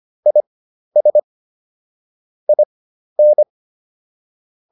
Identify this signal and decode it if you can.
Morse code, heard as a keyed tone, about 25 wpm, spaced as for 12 wpm, 600 Hz. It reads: IS IN